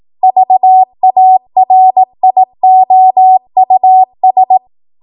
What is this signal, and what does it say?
Morse code, heard as a keyed tone, about 18 wpm, 750 Hz. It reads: VARIOUS
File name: Morse / 18WPM